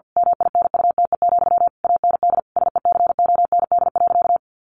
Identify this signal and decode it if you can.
Morse code, heard as a keyed tone, about 50 wpm, 700 Hz. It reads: MINUTE, AND SECOND.
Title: A Sound Of Thunder 50wpm